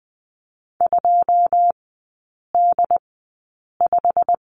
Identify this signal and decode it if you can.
Morse code, heard as a keyed tone, about 20 wpm, spaced as for 8 wpm, 690 Hz. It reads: 2D5
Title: 093 Lesson 16e - Letter c - Three Characters 20wpm